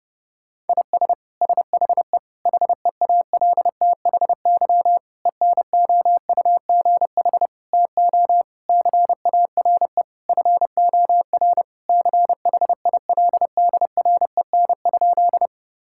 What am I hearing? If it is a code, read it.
Morse code, heard as a keyed tone, about 30 wpm, 700 Hz. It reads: IS SHE HEALTHY ENOUGH TO CARE FOR CHILDREN?
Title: Sentences from Top 1000 Words - Review Part 5 - 30wpm